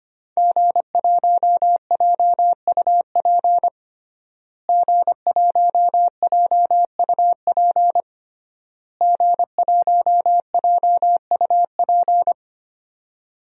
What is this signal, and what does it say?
Morse code, heard as a keyed tone, about 25 wpm, 700 Hz. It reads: G1JUP G1JUP G1JUP
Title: DXCC-100 Call Signs - 5 Characters - Repeated 3x - 25wpm